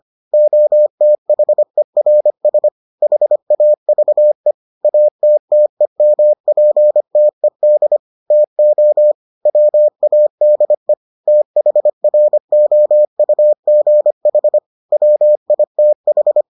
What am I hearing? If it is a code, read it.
Morse code, heard as a keyed tone, about 25 wpm, 600 Hz. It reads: OTHERS HAVE ATTEMPTED TO WADE THROUGH WITH